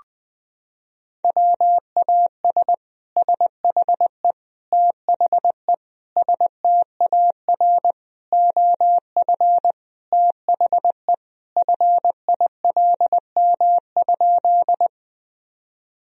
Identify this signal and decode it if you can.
Morse code, heard as a keyed tone, about 20 wpm, 700 Hz. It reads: WAS SHE THE STAR OF THE FILM?
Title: Sentences from Top 1000 Words - Review Part 3 - 20wpm